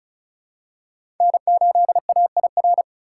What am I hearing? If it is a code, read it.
Morse code, heard as a keyed tone, about 36 wpm, 700 Hz. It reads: N8AIR
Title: USA Call Signs - 5 Characters - 3x Character Spacing - 35wpm